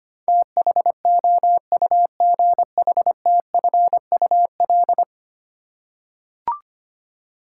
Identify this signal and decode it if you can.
Morse code, heard as a keyed tone, about 25 wpm, 700 Hz. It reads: THOUGHTFUL
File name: 10 Letter Words - First 3 Letters Given - 25wpm